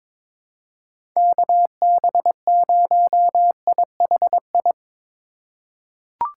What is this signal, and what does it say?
Morse code, heard as a keyed tone, about 22 wpm, 700 Hz. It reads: KB0IHI
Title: USA Call Signs - 6 Characters - 2x Character Spacing - 22wpm